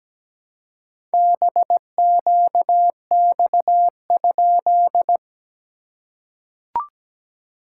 Morse code, heard as a keyed tone, about 17 wpm, 700 Hz. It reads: BQX?
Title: SOTA QSO Practice 17wpm